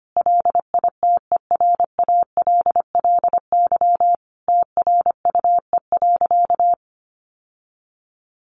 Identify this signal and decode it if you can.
Morse code, heard as a keyed tone, about 25 wpm, 700 Hz. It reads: LITERALLY TRUE.
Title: Sentences from Top 1400 Words 25wpm